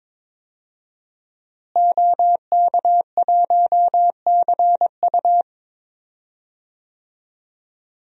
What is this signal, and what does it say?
Morse code, heard as a keyed tone, about 23 wpm, 700 Hz. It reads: OK1CU
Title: DXCC-100 Call Signs - 5 Characters - Repeated 2x - 22wpm